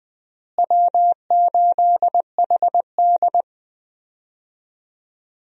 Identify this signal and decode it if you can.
Morse code, heard as a keyed tone, about 20 wpm, 700 Hz. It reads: W8HD
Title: USA Call Signs - 4 Characters - Repeated 3x - 20wpm